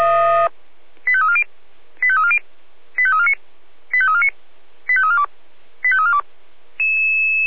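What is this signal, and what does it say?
Five-tone selective call (ZVEI), heard as quick runs of five tones, tones of about 70 ms. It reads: 85340 85340 85349 85349 85342 85342
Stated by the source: Five-tone ZVEI selective call received by SDR.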